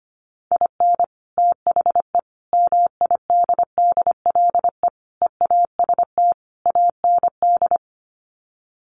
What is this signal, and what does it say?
Morse code, heard as a keyed tone, about 25 wpm, 700 Hz. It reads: IN THE MIDDLE EAST AND
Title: Sets of 5 Words Encore 25wpm